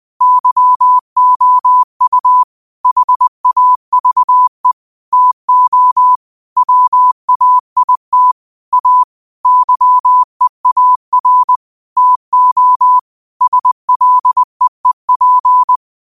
Morse code, heard as a keyed tone, about 20 wpm, 1000 Hz. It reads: YOU HAVE TO WAIT A YEAR TO SLEEP
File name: qotdfiles / 20wpm